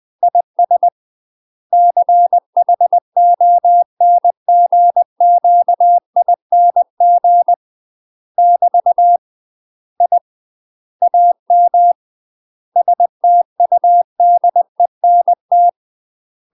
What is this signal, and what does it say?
Morse code, heard as a keyed tone, about 20 wpm, 700 Hz. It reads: IS CHONGQING <BT> I AM STUDENT
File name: UK MASTER QSOs TRAINING / 20WPM / UK MASTER QSOs 0025